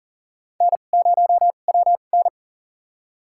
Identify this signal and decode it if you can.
Morse code, heard as a keyed tone, about 40 wpm, spaced as for 29 wpm, 700 Hz. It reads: N0WN
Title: USA Call Signs - 4 Characters - 2x Character Spacing - 40wpm